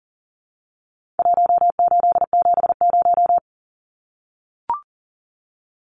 Morse code, heard as a keyed tone, about 40 wpm, 700 Hz. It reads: 1870
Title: Numbers - Contest Style - 4-Digit 40wpm